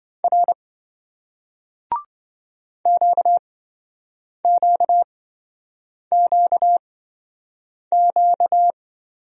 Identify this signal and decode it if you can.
Morse code, heard as a keyed tone, about 27 wpm, spaced as for 7 wpm, 700 Hz. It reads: R QQQQ